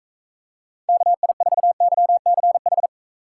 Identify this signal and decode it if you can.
Morse code, heard as a keyed tone, about 41 wpm, 700 Hz. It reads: KI4YCH